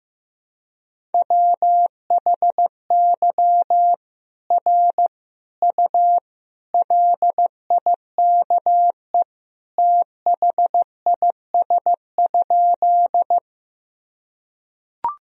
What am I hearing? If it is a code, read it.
Morse code, heard as a keyed tone, about 15 wpm, 700 Hz. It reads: WHY R U LIKE THIS?
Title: Sentences from Top 200 Words and Abbreviations 15wpm